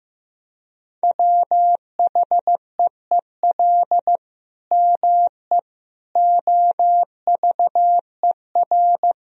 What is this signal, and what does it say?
Morse code, heard as a keyed tone, about 15 wpm, 700 Hz. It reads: WHEEL ME OVER
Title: Sentences from Top 1000 Words - Review Part 3 - 15wpm